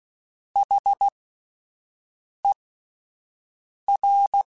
Morse code, heard as a keyed tone, about 16 wpm, spaced as for 5 wpm, 800 Hz. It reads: HER
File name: S vs H / S vs H mp3s / S vs H @5